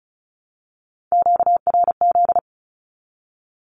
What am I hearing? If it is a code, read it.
Morse code, heard as a keyed tone, about 34 wpm, 700 Hz. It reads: QRZ
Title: Q-Codes 35wpm